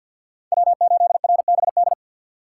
Morse code, heard as a keyed tone, about 52 wpm, 700 Hz. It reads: W8RBD
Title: USA Call Signs - 3x Character Spacing - 50wpm